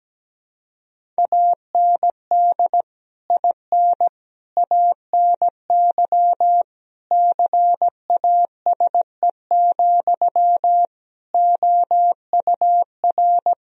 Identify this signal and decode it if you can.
Morse code, heard as a keyed tone, about 17 wpm, 700 Hz. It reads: AND IN ANY CASE, OUR
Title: Sentences from Top 1000 Words - Review Part 1 - 17wpm